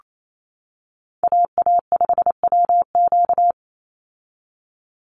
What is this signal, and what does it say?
Morse code, heard as a keyed tone, about 28 wpm, 700 Hz. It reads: AA5WQ
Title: USA Call Signs 28wpm